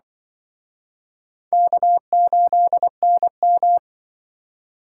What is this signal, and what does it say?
Morse code, heard as a keyed tone, about 25 wpm, 700 Hz. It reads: K8NM
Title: Speed Racing - USA 1x2 Call Signs 20wpm